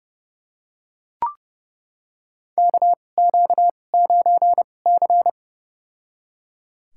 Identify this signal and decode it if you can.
Morse code, heard as a keyed tone, about 31 wpm, spaced as for 22 wpm, 700 Hz. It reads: KQ9C
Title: USA Call Signs - 4 Characters - 2x Character Spacing - 30wpm